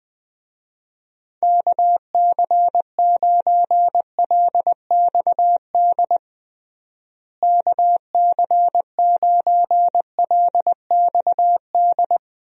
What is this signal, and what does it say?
Morse code, heard as a keyed tone, about 20 wpm, 700 Hz. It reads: KC9LXD KC9LXD
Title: USA Call Signs - 6 Characters - Repeated 3x - 20wpm